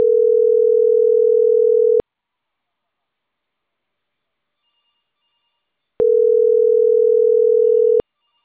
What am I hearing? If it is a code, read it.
Ringback tone (North America), 440 + 480 Hz.
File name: CENTRAL CAEA / out agcaea01 - .26571